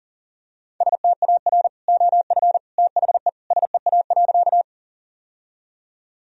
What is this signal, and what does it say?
Morse code, heard as a keyed tone, about 40 wpm, 700 Hz. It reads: STAR OF THE SEA.
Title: Sentences from Top 1000 Words - Review Part 2 - 40wpm